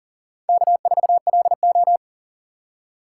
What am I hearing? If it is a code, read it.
Morse code, heard as a keyed tone, about 40 wpm, 700 Hz. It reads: K4LO
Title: USA Call Signs - 3x Character Spacing - 40wpm